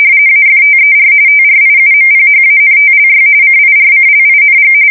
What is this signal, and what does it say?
RTTY, about 45.45 baud. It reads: AFTER TRIUMPHANTLY CIRCLING
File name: fsk170 / battleshipnj